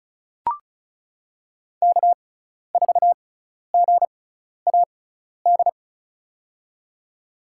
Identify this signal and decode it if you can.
Morse code, heard as a keyed tone, about 36 wpm, spaced as for 12 wpm, 700 Hz. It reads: K4GAD
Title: USA Call Signs - 5 Characters - 6x Character Spacing - 35wpm